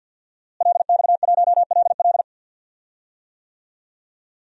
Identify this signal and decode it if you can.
Morse code, heard as a keyed tone, about 53 wpm, 700 Hz. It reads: LX1LL
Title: DXCC-100 Call Signs - 5 Characters - Repeated 2x - 50wpm